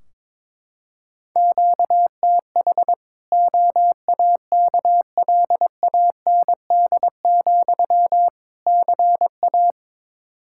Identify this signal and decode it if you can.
Morse code, heard as a keyed tone, about 22 wpm, 700 Hz. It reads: QTH OAKLAND, CA
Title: K1USN SST QSO Practice 22wpm